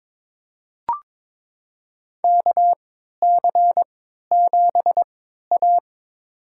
Morse code, heard as a keyed tone, about 22 wpm, spaced as for 13 wpm, 700 Hz. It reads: KC7A
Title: USA Call Signs - 4 Characters - 3x Character Spacing - 22wpm